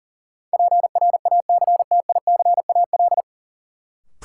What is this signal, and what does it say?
Morse code, heard as a keyed tone, about 37 wpm, 700 Hz. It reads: PRACTICAL
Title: Sentences from Top 1400 Words - New Word Drill Alt - 40wpm